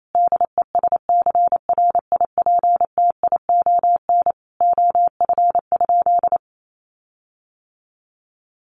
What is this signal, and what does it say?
Morse code, heard as a keyed tone, about 28 wpm, 700 Hz. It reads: DESCRIPTION OF?
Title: Sentences from Top 1300 Words 28wpm